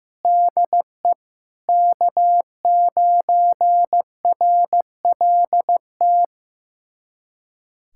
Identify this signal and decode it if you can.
Morse code, heard as a keyed tone, about 15 wpm, 700 Hz. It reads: DE K9RLT
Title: USA Callsign DE Callsign - 5 Characters - 15wpm